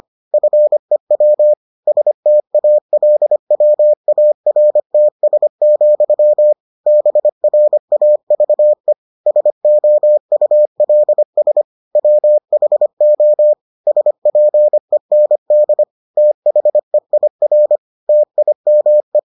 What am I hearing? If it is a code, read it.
Morse code, heard as a keyed tone, about 25 wpm, 600 Hz. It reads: FEW STALWARTS, BRAVE SOULS WHO SPEND THEIR TIME